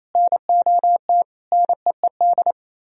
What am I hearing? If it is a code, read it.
Morse code, heard as a keyed tone, about 37 wpm, 700 Hz. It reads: NOT NEED